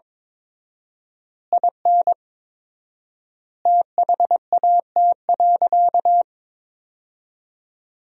Morse code, heard as a keyed tone, about 22 wpm, 700 Hz. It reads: IN THAT.
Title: 4x Word-Spacing - Sentences from Top 200 Words - 22wpm